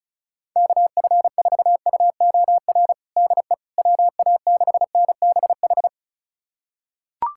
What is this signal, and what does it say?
Morse code, heard as a keyed tone, about 36 wpm, 700 Hz. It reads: KF4UOR DE WA6NBH
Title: USA Callsign DE Callsign - 6 Characters - 35wpm